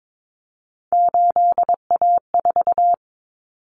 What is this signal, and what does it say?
Morse code, heard as a keyed tone, about 22 wpm, 700 Hz. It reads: 8A4
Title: Common Cut Numbers - 3-Digit 22wpm